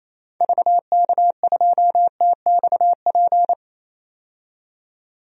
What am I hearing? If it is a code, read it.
Morse code, heard as a keyed tone, about 28 wpm, 700 Hz. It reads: VK2TXP
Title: DXCC-100 Call Signs - 6 Characters - 28wpm